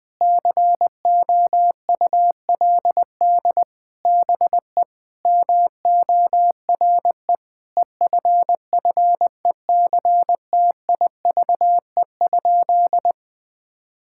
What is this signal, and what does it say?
Morse code, heard as a keyed tone, about 20 wpm, 700 Hz. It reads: COULD BE MORE EFFECTIVE?
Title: Sentences from Top 1000 Words - Review Part 6 - 20wpm